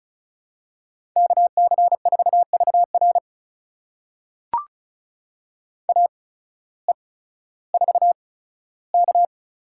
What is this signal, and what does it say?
Morse code, heard as a keyed tone, about 36 wpm, spaced as for 16 wpm, 700 Hz. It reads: KC4VR AE4K
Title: USA Call Signs - 5 Characters - 8x Character Spacing - 35wpm